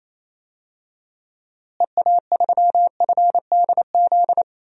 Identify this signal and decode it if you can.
Morse code, heard as a keyed tone, about 28 wpm, 700 Hz. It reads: EA3FDZ
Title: DXCC-100 Call Signs - 6 Characters - Repeated 3x - 28wpm